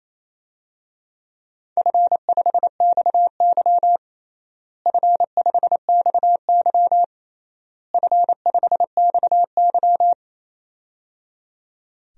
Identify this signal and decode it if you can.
Morse code, heard as a keyed tone, about 28 wpm, 700 Hz. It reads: F5XY F5XY F5XY
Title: DXCC-100 Call Signs - 4 Characters - Repeated 3x - 28wpm